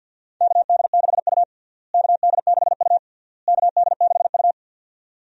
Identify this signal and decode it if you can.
Morse code, heard as a keyed tone, about 49 wpm, 700 Hz. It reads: KD6U KD6U KD6U
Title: USA Call Signs - 4 Characters - Repeated 3x - 50wpm